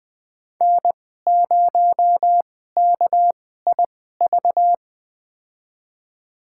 Morse code, heard as a keyed tone, about 20 wpm, spaced as for 15 wpm, 700 Hz. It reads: N0KIV
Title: USA Call Signs - 5 Characters - 2x Character Spacing - 20wpm